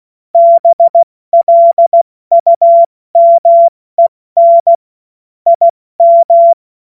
Morse code, heard as a keyed tone, about 16 wpm, 680 Hz. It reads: BLUMEN IM
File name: Witz-02-September-2025-70-F80